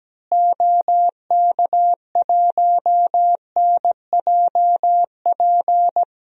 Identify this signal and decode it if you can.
Morse code, heard as a keyed tone, about 17 wpm, 700 Hz. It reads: OK1NJP
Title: DXCC-100 Call Signs - 6 Characters - Repeated 3x - 17wpm